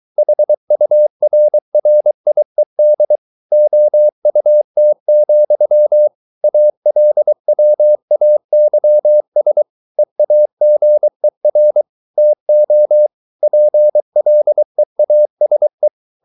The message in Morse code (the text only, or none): HURRIED OUT, ALWAYS EAGER TO PLEASE